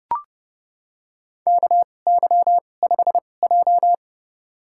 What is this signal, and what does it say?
Morse code, heard as a keyed tone, about 30 wpm, spaced as for 22 wpm, 700 Hz. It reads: KY5J
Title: USA Call Signs - 4 Characters - 2x Character Spacing - 30wpm